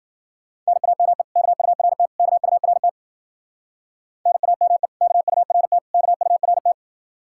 Morse code, heard as a keyed tone, about 52 wpm, 700 Hz. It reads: NAME KURT KURT NAME KURT KURT